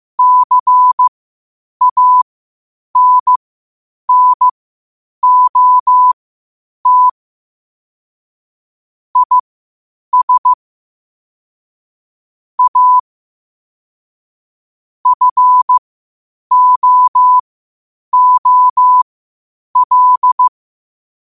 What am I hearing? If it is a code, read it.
Morse code, heard as a keyed tone, about 15 wpm, spaced as for 8 wpm, 1000 Hz. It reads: CANNOT IS A FOOL
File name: qotdfiles / 8wpm